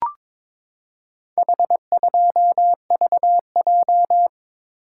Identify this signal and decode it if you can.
Morse code, heard as a keyed tone, about 21 wpm, 700 Hz. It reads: H2VJ
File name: practice / 22wpm / Four Letter-Number Combinations - 22wpm